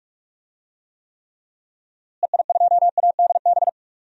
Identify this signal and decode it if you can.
Morse code, heard as a keyed tone, about 41 wpm, 700 Hz. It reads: EI2ADB
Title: DXCC-100 Call Signs - 6 Characters - Repeated 3x - 45wpm